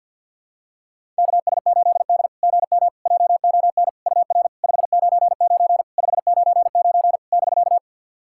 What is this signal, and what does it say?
Morse code, heard as a keyed tone, about 51 wpm, 690 Hz. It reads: KS8D GM JON UR 599 599 <BK>
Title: SOTA QSO Practice 50wpm